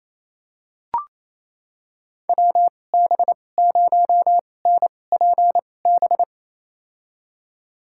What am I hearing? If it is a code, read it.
Morse code, heard as a keyed tone, about 28 wpm, spaced as for 20 wpm, 700 Hz. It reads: WB0NPB
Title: USA Call Signs - 2x Character Spacing - 28wpm